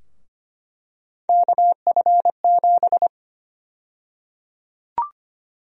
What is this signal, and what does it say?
Morse code, heard as a keyed tone, about 25 wpm, 700 Hz. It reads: KF7
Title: USA Call Sign Prefixes - Repeated 2x - 25wpm